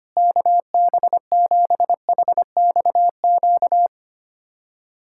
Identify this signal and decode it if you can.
Morse code, heard as a keyed tone, about 25 wpm, 700 Hz. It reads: KB7HXQ
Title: USA Callsign DE Callsign - 6 Characters - 25wpm